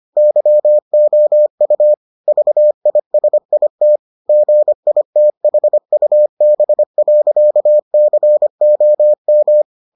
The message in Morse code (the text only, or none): YOU VISIT GITHUB.COM